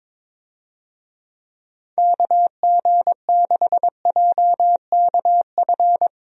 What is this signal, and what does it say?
Morse code, heard as a keyed tone, about 21 wpm, 700 Hz. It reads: KG6JKF
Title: USA Call Signs - 6 Characters - Repeated 2x - 22wpm